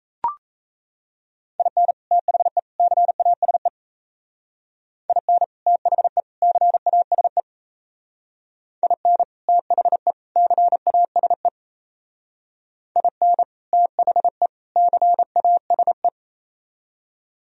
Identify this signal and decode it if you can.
Morse code, heard as a keyed tone, about 34 wpm, 700 Hz. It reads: IN THE CASE IN THE CASE IN THE CASE IN THE CASE
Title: Speed-Racing - Sets of 3 Words 28wpm